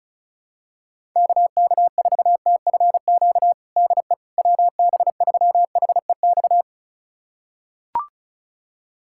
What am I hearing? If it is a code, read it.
Morse code, heard as a keyed tone, about 34 wpm, 700 Hz. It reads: KK4TFQ DE WB3HEX
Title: USA Callsign DE Callsign - 6 Characters - 35wpm